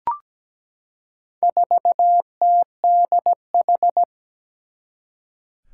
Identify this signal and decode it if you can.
Morse code, heard as a keyed tone, about 17 wpm, 700 Hz. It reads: 4TDH
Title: Four Challenging Letter-Number Combinations 17wpm